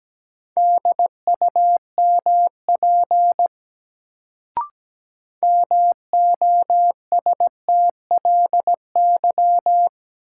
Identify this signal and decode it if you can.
Morse code, heard as a keyed tone, about 17 wpm, 700 Hz. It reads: DUMP MOSTLY
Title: Sentences from Top 1300 Words - New Word Drill - 17wpm